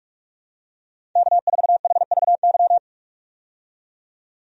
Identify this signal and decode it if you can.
Morse code, heard as a keyed tone, about 44 wpm, 700 Hz. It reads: K4HVY
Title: USA Call Signs - 5 Characters - 45wpm